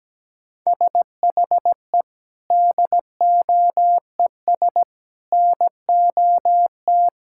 Morse code, heard as a keyed tone, about 17 wpm, 700 Hz. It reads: SHE DOES NOT
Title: Sentences from Top 1000 Words - Review Part 5 - 17wpm